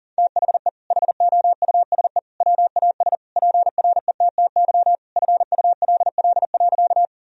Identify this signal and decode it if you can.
Morse code, heard as a keyed tone, about 40 wpm, 700 Hz. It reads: THE HOUSE WAS PRETTY FULL.
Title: Sentences from Top 1000 Words - Review Part 3 - 40wpm